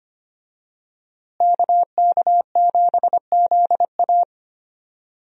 Morse code, heard as a keyed tone, about 25 wpm, 700 Hz. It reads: KK7ZA
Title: USA Call Signs - 5 Characters - Repeated 3x - 25wpm